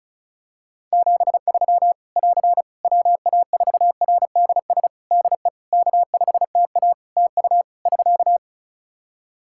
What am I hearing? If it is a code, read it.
Morse code, heard as a keyed tone, about 36 wpm, 700 Hz. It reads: 73 <AR> WA4RDS DE K5TA TU <SK>
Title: Practice QSO - Exchange 4 - Ending - 35wpm